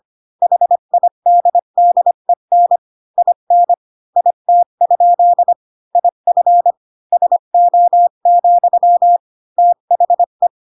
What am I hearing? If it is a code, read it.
Morse code, heard as a keyed tone, about 25 wpm, 700 Hz. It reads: HIDDEN IN IT? IF SO, THE
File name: Pinocchio / Pinocchio ch1 20 25 ik1hge